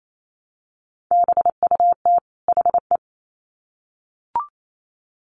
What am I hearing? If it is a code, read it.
Morse code, heard as a keyed tone, about 28 wpm, 700 Hz. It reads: BUT HE
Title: Sets of 2 Words 28wpm